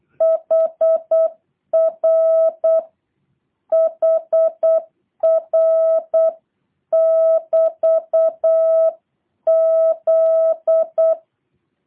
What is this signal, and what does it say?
Morse code, heard as a keyed tone, about 8 wpm, 650 Hz. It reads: HRHR<BT>Z